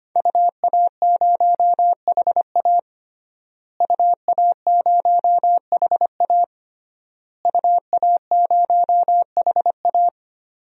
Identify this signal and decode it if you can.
Morse code, heard as a keyed tone, about 25 wpm, 700 Hz. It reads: UA0HA UA0HA UA0HA
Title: DXCC-100 Call Signs - 5 Characters - Repeated 3x - 25wpm